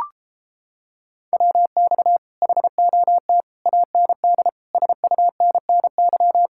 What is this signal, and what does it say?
Morse code, heard as a keyed tone, about 31 wpm, 700 Hz. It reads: WX HOT AND SUNNY